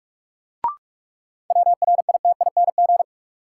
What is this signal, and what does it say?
Morse code, heard as a keyed tone, about 45 wpm, 700 Hz. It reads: WRITING
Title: Sentences from Top 800 Words - New Word Drill - 45wpm